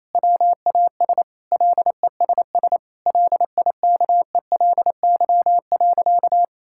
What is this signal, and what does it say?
Morse code, heard as a keyed tone, about 28 wpm, 700 Hz. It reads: WAS LESS LIKELY.
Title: Sentences from Top 1600 Words 28wpm